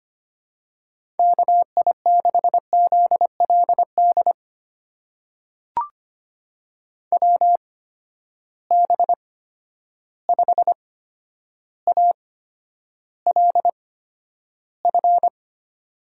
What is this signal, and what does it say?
Morse code, heard as a keyed tone, about 25 wpm, spaced as for 11 wpm, 700 Hz. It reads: KI6ZLD WB5ALF
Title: USA Call Signs - 6 Characters - 8x Character Spacing - 25wpm